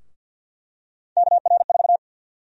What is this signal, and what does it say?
Morse code, heard as a keyed tone, about 51 wpm, 700 Hz. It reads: KR4
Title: USA Call Sign Prefixes - 50wpm